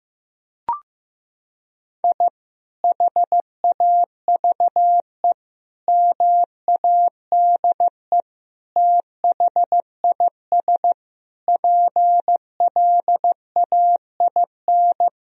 Morse code, heard as a keyed tone, about 15 wpm, 700 Hz. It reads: I HAVE MADE THIS PLAIN
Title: Sentences from Top 1000 Words - Review Part 2 - 15wpm